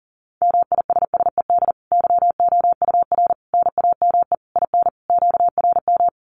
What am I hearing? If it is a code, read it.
Morse code, heard as a keyed tone, about 40 wpm, 700 Hz. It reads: MISSED YOUR NAME IN QRM
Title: Common QSO Phrases 40wpm